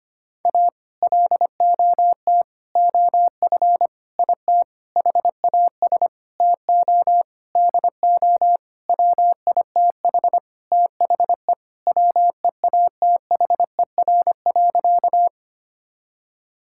Morse code, heard as a keyed tone, about 25 wpm, 700 Hz. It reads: A LOT OF IT HAS TO DO WITH THE WEATHER.
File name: practice / 25wpm / Sentences with 9-Word Phrases - 25wpm